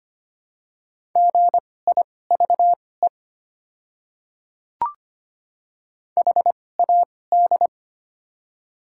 Morse code, heard as a keyed tone, about 25 wpm, spaced as for 18 wpm, 700 Hz. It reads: GIVE HAD
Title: 2x Character Spacing - Top 200 Words - 25wpm